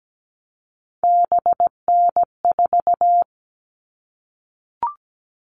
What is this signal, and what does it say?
Morse code, heard as a keyed tone, about 17 wpm, 700 Hz. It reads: BN4
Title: All Cut Numbers - 3-Digit 17wpm